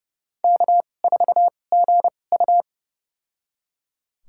Morse code, heard as a keyed tone, about 30 wpm, spaced as for 22 wpm, 700 Hz. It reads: K4GU
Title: USA Call Signs - 4 Characters - 2x Character Spacing - 30wpm